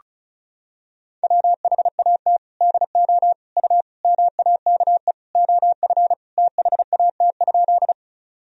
Morse code, heard as a keyed tone, about 36 wpm, 700 Hz. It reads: WHAT DO U MAKE OF THAT?
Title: Sentences from Top 100 Words and Abbreviations 35wpm